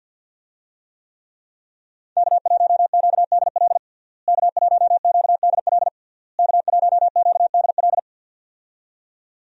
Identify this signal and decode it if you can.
Morse code, heard as a keyed tone, about 52 wpm, 700 Hz. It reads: K1QDL K1QDL K1QDL
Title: USA Call Signs - 5 Characters - Repeated 3x - 50wpm